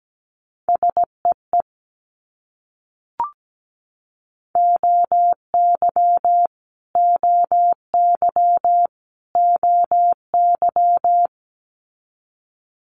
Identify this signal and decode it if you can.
Morse code, heard as a keyed tone, about 17 wpm, 700 Hz. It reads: SEE OY OY OY
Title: Multi-letter Phonemes Repeated 3x 17wpm